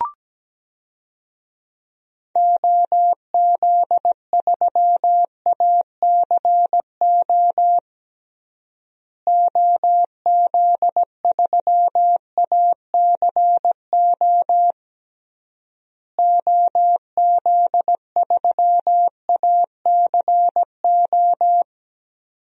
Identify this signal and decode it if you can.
Morse code, heard as a keyed tone, about 17 wpm, 700 Hz. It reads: OZ3ACO OZ3ACO OZ3ACO
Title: DXCC-100 Call Signs - 6 Characters - Repeated 3x - 17wpm